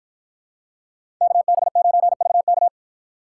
Morse code, heard as a keyed tone, about 50 wpm, 700 Hz. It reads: KB9VK